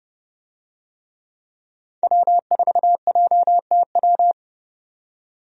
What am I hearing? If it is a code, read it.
Morse code, heard as a keyed tone, about 29 wpm, 700 Hz. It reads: W4JTW
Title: USA Call Signs - 5 Characters - 30wpm